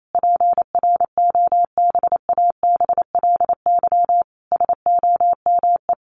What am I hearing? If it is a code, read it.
Morse code, heard as a keyed tone, about 28 wpm, 700 Hz. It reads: PROBABLY SOME
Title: Sentences from Top 1100 Words 28wpm